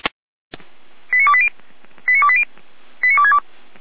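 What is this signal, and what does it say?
Five-tone selective call (ZVEI), heard as quick runs of five tones, tones of about 70 ms. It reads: 89280 89280 89262
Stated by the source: ZVEI five-tone selective call received on an SDR.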